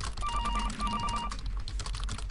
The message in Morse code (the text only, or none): none